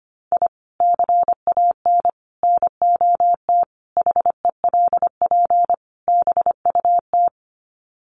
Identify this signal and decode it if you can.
Morse code, heard as a keyed tone, about 25 wpm, 700 Hz. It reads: I CAN NOT HELP BUT
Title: Sets of 5 Words Encore2 25wpm